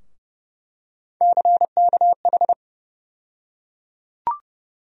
Morse code, heard as a keyed tone, about 30 wpm, 700 Hz. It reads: CKH
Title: USA Call Sign Postfixes - 3 Letters 30wpm